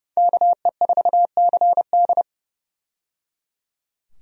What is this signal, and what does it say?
Morse code, heard as a keyed tone, about 30 wpm, 700 Hz. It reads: KE4CD
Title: USA Call Signs - 5 Characters - 30wpm